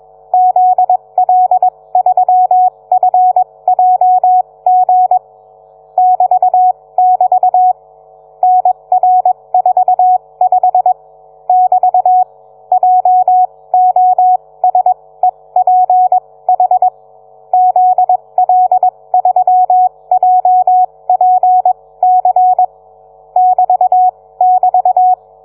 Morse code, heard as a keyed tone, about 21 wpm, spaced as for 18 wpm, 740 Hz. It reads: ZL3FJG <BT><BT> NR45 <BT> JOSEPH ZL3JPC <BT><BT>